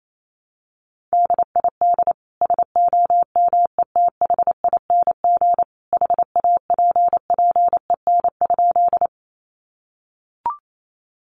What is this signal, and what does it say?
Morse code, heard as a keyed tone, about 28 wpm, 700 Hz. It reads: DID SOMETHING HAPPEN?
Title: Sentences from Top 600 Words 28wpm